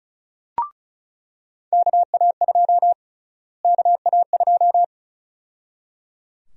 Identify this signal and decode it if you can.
Morse code, heard as a keyed tone, about 34 wpm, 700 Hz. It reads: KA2 KA2
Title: USA Call Sign Prefixes - Repeated 2x - 35wpm